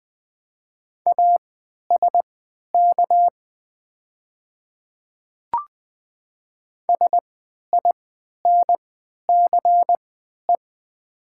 Morse code, heard as a keyed tone, about 20 wpm, spaced as for 11 wpm, 700 Hz. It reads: ASK SINCE
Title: 3x Character Spacing - Top 500 Words - 20wpm